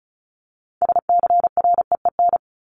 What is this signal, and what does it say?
Morse code, heard as a keyed tone, about 34 wpm, 700 Hz. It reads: SCREEN
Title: Top 751-1000 Words 35wpm